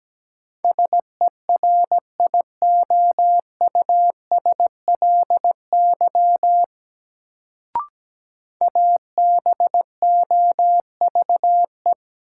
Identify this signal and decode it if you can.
Morse code, heard as a keyed tone, about 17 wpm, 700 Hz. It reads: SERIOUSLY ABOVE
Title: Sentences from Top 1200 Words - New Word Drill Alt - 17wpm